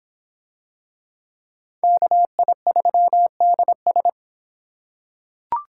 Morse code, heard as a keyed tone, about 25 wpm, 700 Hz. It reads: KI3DS